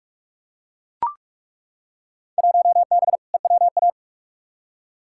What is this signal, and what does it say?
Morse code, heard as a keyed tone, about 42 wpm, 700 Hz. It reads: 1B EWA
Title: Field Day Exchanges Encore - 45wpm